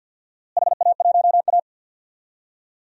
Morse code, heard as a keyed tone, about 51 wpm, 700 Hz. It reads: HA1A
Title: DXCC-100 Call Signs - 4 Characters - Repeated 2x - 50wpm